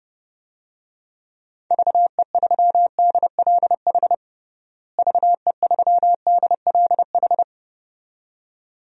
Morse code, heard as a keyed tone, about 29 wpm, 700 Hz. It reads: VE3DLH VE3DLH
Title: DXCC-100 Call Signs - 6 Characters - Repeated 2x - 30wpm